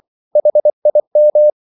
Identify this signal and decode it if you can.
Morse code, heard as a keyed tone, about 24 wpm, 600 Hz. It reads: HIM
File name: CW / Practice / EnglishQuotations0224